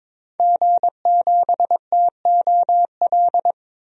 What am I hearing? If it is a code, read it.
Morse code, heard as a keyed tone, about 22 wpm, 700 Hz. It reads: G7TOL
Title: DXCC-100 Call Signs - 5 Characters - 22wpm